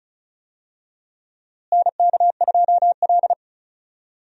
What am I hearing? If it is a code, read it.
Morse code, heard as a keyed tone, about 34 wpm, 700 Hz. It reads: NK2L
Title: USA Call Signs - 4 Characters - Repeated 3x - 35wpm